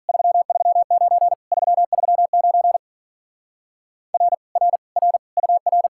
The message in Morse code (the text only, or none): 339 339 R R R UR